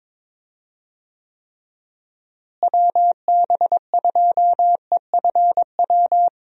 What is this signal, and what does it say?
Morse code, heard as a keyed tone, about 22 wpm, 700 Hz. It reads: WB2EFW
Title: USA Call Signs - 6 Characters - Repeated 3x - 22wpm